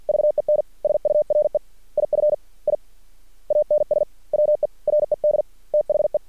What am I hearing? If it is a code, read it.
Morse code, heard as a keyed tone, about 50 wpm, spaced as for 42 wpm, 600 Hz. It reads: VEN SURE IF I ANSWERED THE
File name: archived / CRSnet